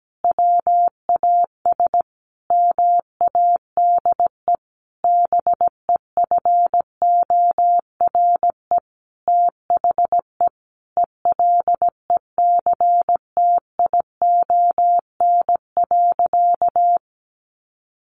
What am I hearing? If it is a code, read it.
Morse code, heard as a keyed tone, about 17 wpm, 700 Hz. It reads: WAS MADE BEFORE THE ELECTION.
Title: Sentences from Top 1500 Words 17wpm